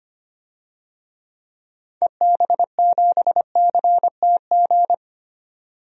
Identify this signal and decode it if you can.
Morse code, heard as a keyed tone, about 25 wpm, 700 Hz. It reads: EB7CTG
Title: DXCC-100 Call Signs - 6 Characters - 25wpm